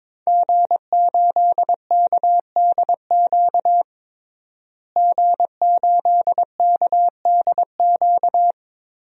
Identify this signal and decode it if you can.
Morse code, heard as a keyed tone, about 22 wpm, 700 Hz. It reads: G8KDQ G8KDQ
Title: DXCC-100 Call Signs - 5 Characters - Repeated 2x - 22wpm